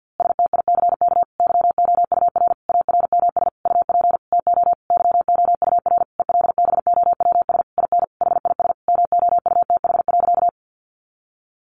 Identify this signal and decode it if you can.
Morse code, heard as a keyed tone, about 50 wpm, 700 Hz. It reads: STICK YOUR ARMS UP TO YOUR ELBOWS IN HIS MOUTH.
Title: A Sound Of Thunder 50wpm